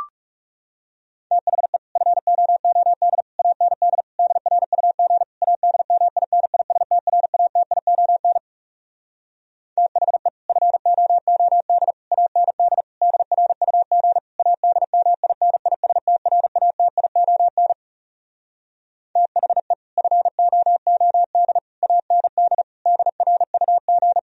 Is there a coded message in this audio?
Yes: Morse code from 1.3 to 24.2 s, heard as a keyed tone, about 40 wpm, 700 Hz. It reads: THE FOOD AND DRUG ADMINISTRATION THE FOOD AND DRUG ADMINISTRATION THE FOOD AND DRUG